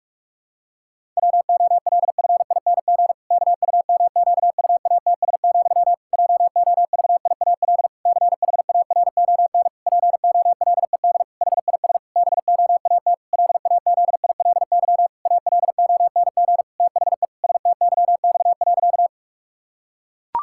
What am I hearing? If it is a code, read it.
Morse code, heard as a keyed tone, about 46 wpm, 700 Hz. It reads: WOLFING KUMQUATS, JOVIAL CHARON POLED HIS BOAT LAZILY ALONG THE STYX.
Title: Pangram Challenge v2 - 45wpm